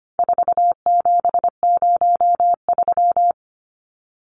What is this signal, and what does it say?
Morse code, heard as a keyed tone, about 25 wpm, 700 Hz. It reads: 4703
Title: Numbers - 4-Digit 25wpm